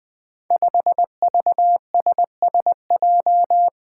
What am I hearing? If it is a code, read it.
Morse code, heard as a keyed tone, about 20 wpm, 700 Hz. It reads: 5VSSJ
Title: Five Challenging Letter-Number Combinations 20wpm